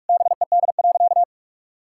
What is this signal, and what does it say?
Morse code, heard as a keyed tone, about 41 wpm, 700 Hz. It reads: BED.
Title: Sentences from Top 1000 Words - Review Part 5 - 45wpm